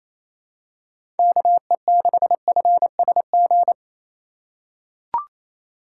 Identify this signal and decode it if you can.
Morse code, heard as a keyed tone, about 28 wpm, 700 Hz. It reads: KE6FSG
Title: USA Call Signs - 4x Character Spacing - 28wpm